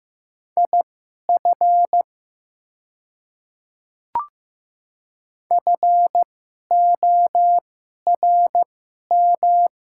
Morse code, heard as a keyed tone, about 15 wpm, spaced as for 11 wpm, 700 Hz. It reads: IF FORM
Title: 2x Character Spacing - Top 200 Words - 15wpm